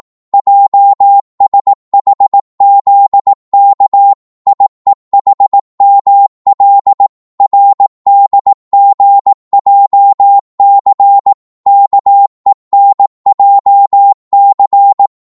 Morse code, heard as a keyed tone, about 18 wpm, 800 Hz. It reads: JSHZK IEHML RDGJC KENJC
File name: Morse / ltr-18wpm